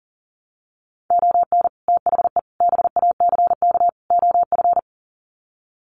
Morse code, heard as a keyed tone, about 40 wpm, 700 Hz. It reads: ON THE BACK OF
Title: Sets of 4 Words 40wpm